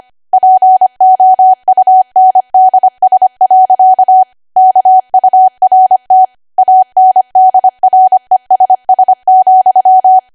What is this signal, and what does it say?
Morse code, heard as a keyed tone, about 25 wpm, 750 Hz. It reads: POUNDS. KURT ANDRESS,
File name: Archive / 25 WPM / 25WPM